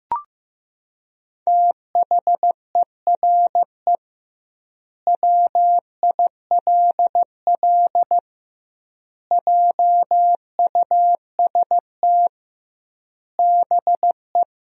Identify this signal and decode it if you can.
Morse code, heard as a keyed tone, about 16 wpm, 700 Hz. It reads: THERE WILL JUST BE
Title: 2x Word-Spacing - Sentences from Top 200 Words - 15wpm